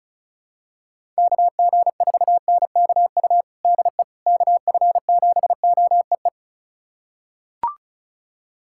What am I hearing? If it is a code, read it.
Morse code, heard as a keyed tone, about 36 wpm, 700 Hz. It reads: KG4NKU DE KF7OEE
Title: USA Callsign DE Callsign - 6 Characters - 35wpm